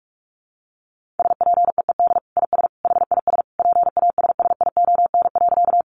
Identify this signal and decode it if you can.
Morse code, heard as a keyed tone, about 44 wpm, 700 Hz. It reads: SPEED IS HIS PASSION.
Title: Sentences from Top 1300 Words 45wpm